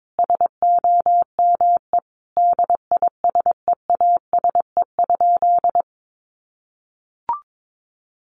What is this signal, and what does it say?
Morse code, heard as a keyed tone, about 22 wpm, 700 Hz. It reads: SOME DISEASE?
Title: Sentences from Top 1500 Words 22wpm